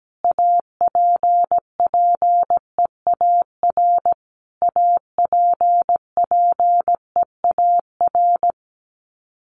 Morse code, heard as a keyed tone, about 17 wpm, 700 Hz. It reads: APPEAR APPEAR
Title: Top 500 Words Repeated 17wpm